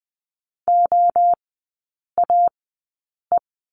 Morse code, heard as a keyed tone, about 20 wpm, 700 Hz. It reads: O A E
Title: 035 Lesson 7e - Letter s - Three Characters 20wpm